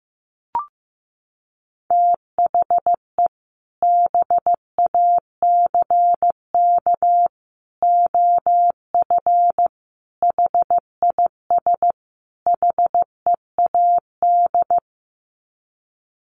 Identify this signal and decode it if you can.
Morse code, heard as a keyed tone, about 15 wpm, 700 Hz. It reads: THE BACK OF HIS HEAD
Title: Sets of 5 Words Encore 15wpm